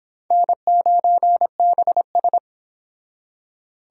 Morse code, heard as a keyed tone, about 26 wpm, 700 Hz. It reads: N9BS